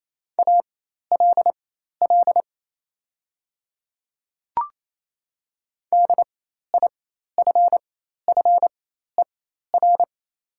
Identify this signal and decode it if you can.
Morse code, heard as a keyed tone, about 28 wpm, spaced as for 13 wpm, 700 Hz. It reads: ALL DIFFER
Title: 4x Character Spacing - Top 500 Words - 28wpm